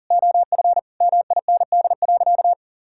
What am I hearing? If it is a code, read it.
Morse code, heard as a keyed tone, about 37 wpm, 700 Hz. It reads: OF MIND.
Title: Sentences from Top 1000 Words - Review Part 6 - 40wpm